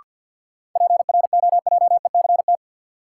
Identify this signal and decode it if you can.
Morse code, heard as a keyed tone, about 46 wpm, 700 Hz. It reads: PROJECT
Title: Sentences from Top 1400 Words - New Word Drill - 50wpm